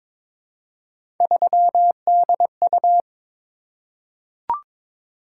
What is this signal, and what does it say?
Morse code, heard as a keyed tone, about 22 wpm, 700 Hz. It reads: 3DU
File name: practice / 22wpm / Three Letter-Number Combinations - 22wpm